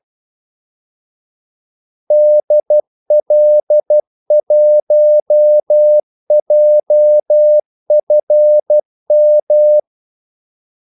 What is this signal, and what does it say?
Morse code, heard as a keyed tone, about 12 wpm, 600 Hz. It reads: DL1JFM
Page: 2 ekstra ordmellemrum mellem kaldesignalerne